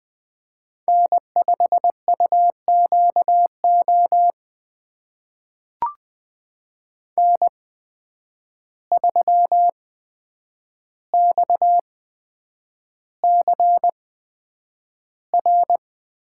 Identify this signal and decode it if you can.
Morse code, heard as a keyed tone, about 20 wpm, spaced as for 9 wpm, 700 Hz. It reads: N5UQO N3XCR
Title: USA Call Signs - 5 Characters - 8x Character Spacing - 20wpm